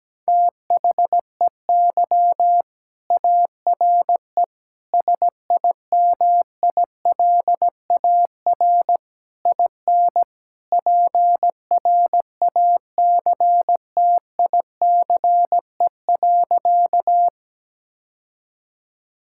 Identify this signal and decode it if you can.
Morse code, heard as a keyed tone, about 17 wpm, 700 Hz. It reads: THEY ARE SIMILAR IN PRACTICE.
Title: Sentences from Top 1000 Words - Review Part 5 - 17wpm